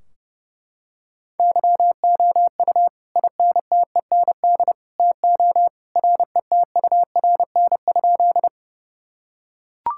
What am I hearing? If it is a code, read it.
Morse code, heard as a keyed tone, about 30 wpm, 700 Hz. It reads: YOU INTEND TO RETURN?
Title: Sentences from Top 1600 Words 30wpm